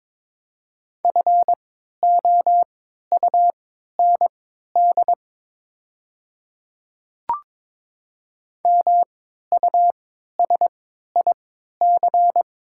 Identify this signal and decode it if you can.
Morse code, heard as a keyed tone, about 22 wpm, spaced as for 13 wpm, 700 Hz. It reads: FOUND MUSIC
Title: 3x Character Spacing - Top 300 Words - 22wpm